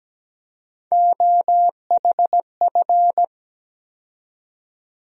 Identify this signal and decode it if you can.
Morse code, heard as a keyed tone, about 17 wpm, 700 Hz. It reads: OHF
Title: USA Call Sign Postfixes - 3 Letters 17wpm